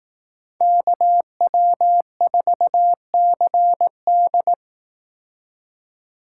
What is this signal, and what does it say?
Morse code, heard as a keyed tone, about 18 wpm, 700 Hz. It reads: KW4CD
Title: Speed Racing - USA 2x2 Call Signs 15wpm